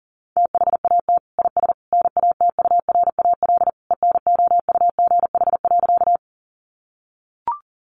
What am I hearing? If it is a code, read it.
Morse code, heard as a keyed tone, about 40 wpm, 700 Hz. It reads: THAT IS NATURAL ENOUGH.
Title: Sentences from Top 700 Words 40wpm